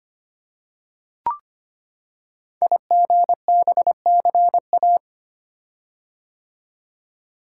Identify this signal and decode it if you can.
Morse code, heard as a keyed tone, about 25 wpm, 700 Hz. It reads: IGBCA
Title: Code Groups - 5 Letters - 25wpm